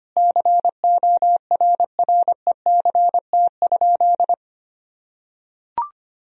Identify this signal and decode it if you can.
Morse code, heard as a keyed tone, about 25 wpm, 700 Hz. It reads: CORRECT?
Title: Sentences from Top 1000 Words - Review Part 2 - 25wpm